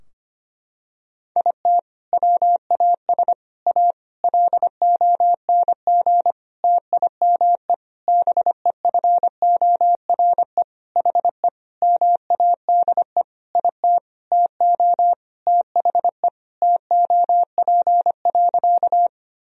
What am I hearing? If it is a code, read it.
Morse code, heard as a keyed tone, about 25 wpm, 700 Hz. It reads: IT WAS A LONG TIME BEFORE HE MADE IT TO THE TOP.
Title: Sentences from Top 1000 Words - Review Part 2 - 25wpm